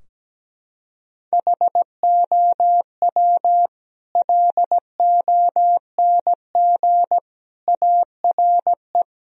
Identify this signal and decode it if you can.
Morse code, heard as a keyed tone, about 17 wpm, 700 Hz. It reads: HOW LONG ARE
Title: Sentences from Top 1000 Words - Review Part 2 - 17wpm